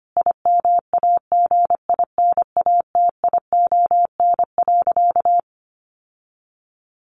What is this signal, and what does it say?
Morse code, heard as a keyed tone, about 25 wpm, 700 Hz. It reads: IMAGINATION.
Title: Sentences from Top 1300 Words 25wpm